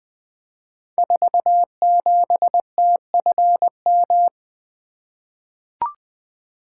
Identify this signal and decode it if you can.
Morse code, heard as a keyed tone, about 20 wpm, 700 Hz. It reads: 47TFM
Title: Code Groups - 5 Letter:Numbers - 20wpm